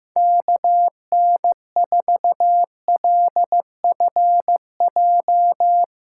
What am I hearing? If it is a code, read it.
Morse code, heard as a keyed tone, about 15 wpm, 700 Hz. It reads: KN4LFJ
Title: USA Call Signs - 6 Characters - 3x Character Spacing - 15wpm